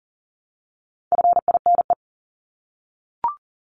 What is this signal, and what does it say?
Morse code, heard as a keyed tone, about 40 wpm, 700 Hz. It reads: FINE
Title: Top 500 Words 40wpm